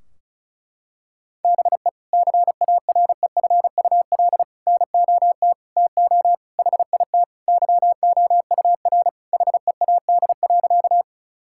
Morse code, heard as a keyed tone, about 36 wpm, 700 Hz. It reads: BE CAREFUL NOT TO HIT YOUR HEAD.
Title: Sentences from Top 1600 Words 35wpm